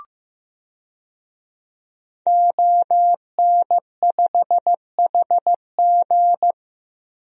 Morse code, heard as a keyed tone, about 15 wpm, 700 Hz. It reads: ON5HG
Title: DXCC-100 Call Signs - 5 Characters - Repeated 3x - 15wpm